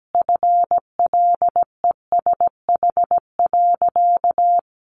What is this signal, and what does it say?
Morse code, heard as a keyed tone, about 17 wpm, 700 Hz. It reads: FLESH.
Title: A Sound Of Thunder 17wpm